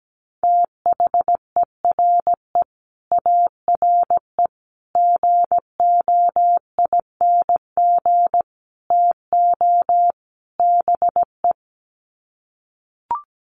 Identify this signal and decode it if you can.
Morse code, heard as a keyed tone, about 17 wpm, 700 Hz. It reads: THERE ARE GOING TO BE
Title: Sets of 5 Words 17wpm